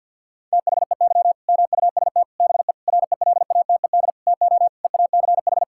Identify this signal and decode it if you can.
Morse code, heard as a keyed tone, about 52 wpm, 700 Hz. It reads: THEY MUST BE RELATED TO EACH
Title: Sentences from Top 1000 Words - Review Part 5 - 50wpm